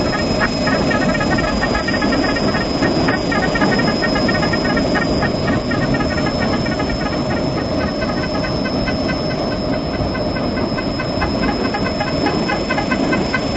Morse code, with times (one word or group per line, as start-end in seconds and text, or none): none